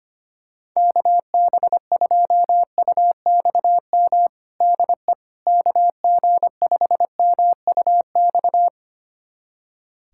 Morse code, heard as a keyed tone, about 25 wpm, 700 Hz. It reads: KB2UXM DE KG5MUX
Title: USA Callsign DE Callsign - 6 Characters - 25wpm